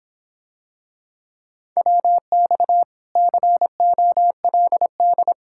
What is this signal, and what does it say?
Morse code, heard as a keyed tone, about 26 wpm, 700 Hz. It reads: WX COLD